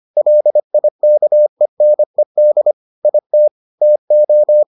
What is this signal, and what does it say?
Morse code, heard as a keyed tone, about 25 wpm, 600 Hz. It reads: LIKENED IT TO